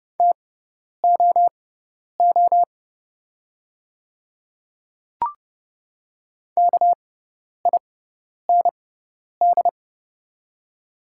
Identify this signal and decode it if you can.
Morse code, heard as a keyed tone, about 30 wpm, spaced as for 10 wpm, 700 Hz. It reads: TOO KIND
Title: 6x Character Spacing - Top 300 Words - 30wpm